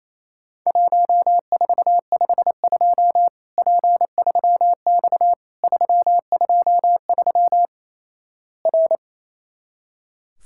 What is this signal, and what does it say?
Morse code, heard as a keyed tone, about 28 wpm, 700 Hz. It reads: 1452 P3X 323 R
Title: Worked All Europe DX Contest - QTC Exchange - 28wpm